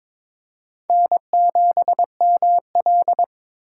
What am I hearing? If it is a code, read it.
Morse code, heard as a keyed tone, about 22 wpm, 700 Hz. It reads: N7ML
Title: K1USN SST QSO Practice - 22wpm